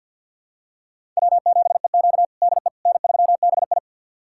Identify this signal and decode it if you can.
Morse code, heard as a keyed tone, about 52 wpm, 700 Hz. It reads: W7EQ DE N3BI
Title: USA Callsign DE Callsign - 4 Characters - 50wpm